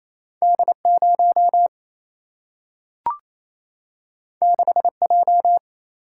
Morse code, heard as a keyed tone, about 28 wpm, 700 Hz. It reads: D 0 6 J
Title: Single-Letter Single-Number 28wpm